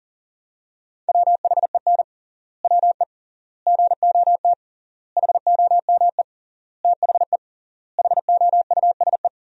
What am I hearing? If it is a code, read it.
Morse code, heard as a keyed tone, about 40 wpm, 700 Hz. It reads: WHEN WE GOT HOME THE HOUSE
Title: 3x Word-Spacing - Sentences from Top 500 Words - 40wpm